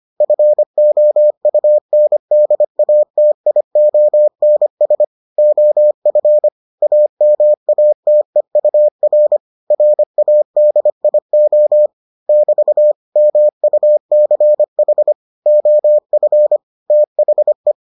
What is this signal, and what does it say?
Morse code, heard as a keyed tone, about 25 wpm, 600 Hz. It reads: FOUNDATIONS OF AMATEUR RADIO <BT> MUCH OF THE